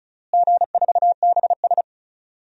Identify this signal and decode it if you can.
Morse code, heard as a keyed tone, about 36 wpm, 700 Hz. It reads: G4BS
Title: DXCC-100 Call Signs - 4 Characters - Repeated 2x - 35wpm